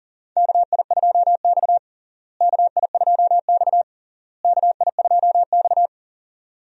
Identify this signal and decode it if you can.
Morse code, heard as a keyed tone, about 40 wpm, 700 Hz. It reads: KI2X KI2X KI2X
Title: USA Call Signs - 4 Characters - Repeated 3x - 40wpm